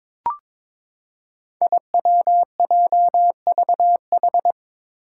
Morse code, heard as a keyed tone, about 22 wpm, 700 Hz. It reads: IWJVH
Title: Five Challenging Letter-Number Combinations 22wpm